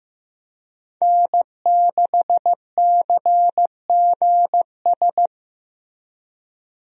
Morse code, heard as a keyed tone, about 15 wpm, 700 Hz. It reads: N6CGS
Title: USA Call Signs - 5 Characters - Repeated 3x - 15wpm